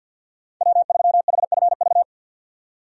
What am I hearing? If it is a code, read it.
Morse code, heard as a keyed tone, about 50 wpm, 700 Hz. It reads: W3HFV
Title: USA Call Signs - 5 Characters - 50wpm